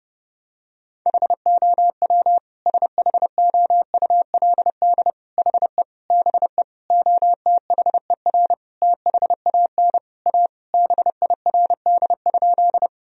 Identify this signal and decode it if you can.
Morse code, heard as a keyed tone, about 32 wpm, 700 Hz. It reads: HOW SHOULD HE BE OTHER THAN A BIRD?
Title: Sentences from Top 1000 Words - Review Part 2 - 30wpm